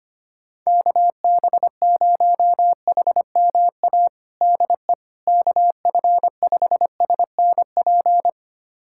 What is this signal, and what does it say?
Morse code, heard as a keyed tone, about 25 wpm, 700 Hz. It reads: KB0HMA DE KF5SNP
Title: USA Callsign DE Callsign - 6 Characters - 25wpm